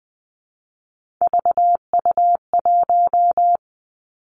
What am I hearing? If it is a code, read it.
Morse code, heard as a keyed tone, about 20 wpm, 700 Hz. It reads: VU1
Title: Three Challenging Letter-Number Combinations 20wpm